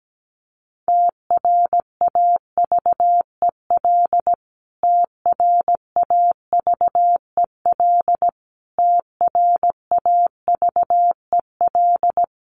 Morse code, heard as a keyed tone, about 17 wpm, 700 Hz. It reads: TRAVEL TRAVEL TRAVEL
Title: Top 500 Words Repeated 17wpm